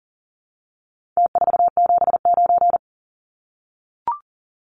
Morse code, heard as a keyed tone, about 39 wpm, 700 Hz. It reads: T479
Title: Numbers - Contest Style - 4-Digit 40wpm